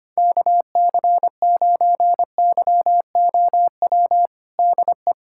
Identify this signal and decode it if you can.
Morse code, heard as a keyed tone, about 25 wpm, 700 Hz. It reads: KC9YOW DE
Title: USA Callsign DE Callsign - 6 Characters - 25wpm